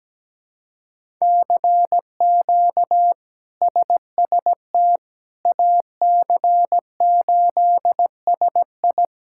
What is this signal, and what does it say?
Morse code, heard as a keyed tone, about 17 wpm, 700 Hz. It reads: CQ SST AC8SI
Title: K1USN SST QSO Practice - 17wpm